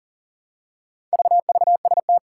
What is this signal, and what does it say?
Morse code, heard as a keyed tone, about 40 wpm, 700 Hz. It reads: VVST
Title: Four Challenging Letter-Number Combinations 40wpm